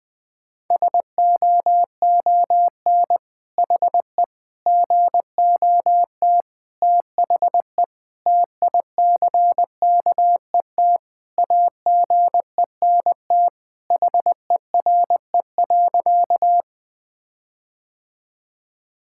Morse code, heard as a keyed tone, about 20 wpm, 700 Hz. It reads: SOON HE GOT THE TICKET AGENT HERE.
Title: Sentences from Top 1600 Words 20wpm